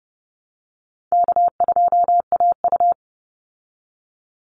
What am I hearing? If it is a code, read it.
Morse code, heard as a keyed tone, about 29 wpm, 700 Hz. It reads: K2AU
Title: Call Signs Repeated 3x - 30wpm